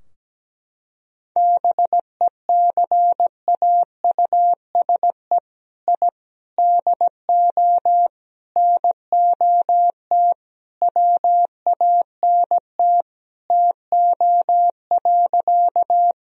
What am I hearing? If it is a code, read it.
Morse code, heard as a keyed tone, about 17 wpm, 700 Hz. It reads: BECAUSE I DO NOT WANT TO.
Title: Sentences from Top 1000 Words - Review Part 3 - 17wpm